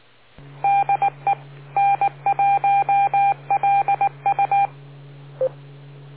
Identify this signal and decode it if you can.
Morse code, heard as a keyed tone, about 19 wpm, 800 Hz. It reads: DE N1LU